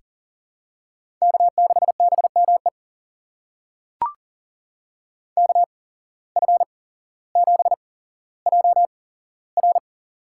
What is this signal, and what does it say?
Morse code, heard as a keyed tone, about 40 wpm, spaced as for 17 wpm, 700 Hz. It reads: K6BME KF7JR
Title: USA Call Signs - 8x Character Spacing - 40wpm